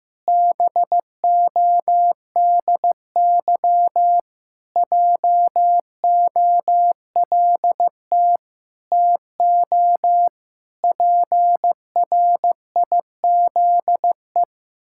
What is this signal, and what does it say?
Morse code, heard as a keyed tone, about 15 wpm, 700 Hz. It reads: BODY JOLT TO PRIZE
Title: Pangram Challenge v2 - 15wpm